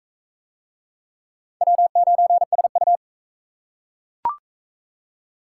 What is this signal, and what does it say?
Morse code, heard as a keyed tone, about 37 wpm, 700 Hz. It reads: W9SU